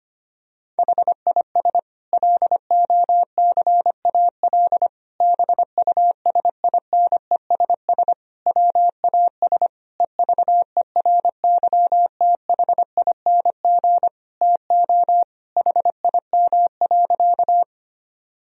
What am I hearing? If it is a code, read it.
Morse code, heard as a keyed tone, about 25 wpm, 700 Hz. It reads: HIS LOCAL BUSINESS WAS EVERYTHING TO HIM.
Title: Sentences from Top 1000 Words - Review Part 3 - 25wpm